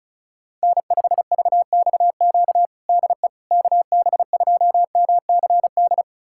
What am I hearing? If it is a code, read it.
Morse code, heard as a keyed tone, about 36 wpm, 700 Hz. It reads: N5VXQ DE KB2MCD
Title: USA Callsign DE Callsign 35wpm